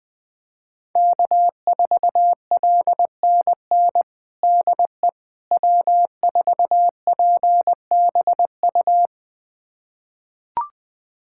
Morse code, heard as a keyed tone, about 20 wpm, 700 Hz. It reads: K4LNN DE W4PBU
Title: USA Callsign DE Callsign - 5 Characters - 20wpm